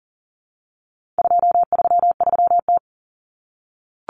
Morse code, heard as a keyed tone, about 40 wpm, 700 Hz. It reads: 233T
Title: Common Cut Numbers - 4-Digit 40wpm